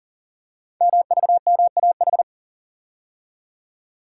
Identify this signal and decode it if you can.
Morse code, heard as a keyed tone, about 38 wpm, 700 Hz. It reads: MVMAH
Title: Code Groups - 5 Letters - 40wpm